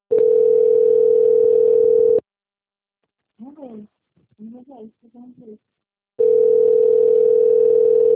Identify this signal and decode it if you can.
Ringback tone (North America), 440 + 480 Hz, with speech.
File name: out- - - - - .13207